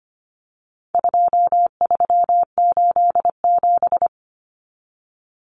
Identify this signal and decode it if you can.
Morse code, heard as a keyed tone, about 25 wpm, 700 Hz. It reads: 2387
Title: Common Cut Numbers - 4-Digit 25wpm